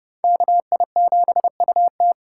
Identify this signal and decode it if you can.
Morse code, heard as a keyed tone, about 26 wpm, 700 Hz. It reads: KI7UT